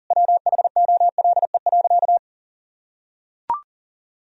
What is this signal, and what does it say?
Morse code, heard as a keyed tone, about 40 wpm, 700 Hz. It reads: WHOLE.
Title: Sentences from Top 1000 Words - Review Part 1 - 40wpm